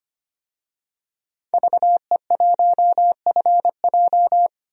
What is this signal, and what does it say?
Morse code, heard as a keyed tone, about 25 wpm, 700 Hz. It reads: VE1FJ
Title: DXCC-100 Call Signs - 5 Characters - 25wpm